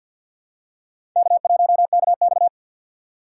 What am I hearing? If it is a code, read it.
Morse code, heard as a keyed tone, about 52 wpm, 700 Hz. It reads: K1KX
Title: USA Call Signs - 4 Characters - 3x Character Spacing - 50wpm